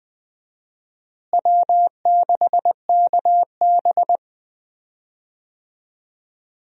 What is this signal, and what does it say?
Morse code, heard as a keyed tone, about 20 wpm, 700 Hz. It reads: W6KB
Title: USA Call Signs - 4 Characters - Repeated 3x - 20wpm